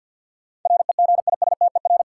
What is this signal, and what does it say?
Morse code, heard as a keyed tone, about 52 wpm, 700 Hz. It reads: REGISTER
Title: Sentences from Top 1200 Words - New Word Drill - 50wpm